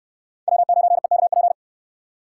Morse code, heard as a keyed tone, about 69 wpm, 700 Hz. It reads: W9EKP